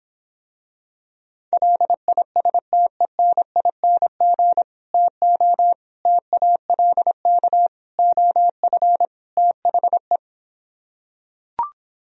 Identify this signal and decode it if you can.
Morse code, heard as a keyed tone, about 26 wpm, 700 Hz. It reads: LISTENING TO TALK OF THE